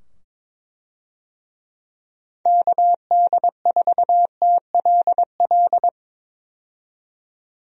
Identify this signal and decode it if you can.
Morse code, heard as a keyed tone, about 22 wpm, 700 Hz. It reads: KD4TLL
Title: USA Call Signs - 6 Characters - 22wpm